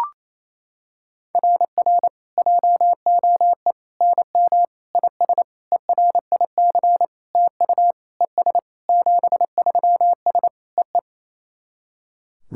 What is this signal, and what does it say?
Morse code, heard as a keyed tone, about 28 wpm, 700 Hz. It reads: RR JOE NM IS ERIC TU ES 73S EE
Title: SOTA QSO Practice 28wpm